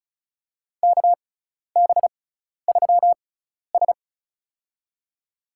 Morse code, heard as a keyed tone, about 36 wpm, spaced as for 12 wpm, 700 Hz. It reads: KB3S
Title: USA Call Signs - 4 Characters - 6x Character Spacing - 35wpm